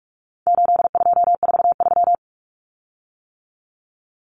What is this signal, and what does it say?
Morse code, heard as a keyed tone, about 44 wpm, 700 Hz. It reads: 8243
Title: Common Cut Numbers - 4-Digit 45wpm